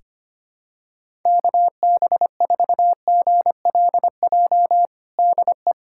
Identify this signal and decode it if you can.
Morse code, heard as a keyed tone, about 25 wpm, 700 Hz. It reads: KB4GLJ DE
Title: USA Callsign DE Callsign - 6 Characters - 25wpm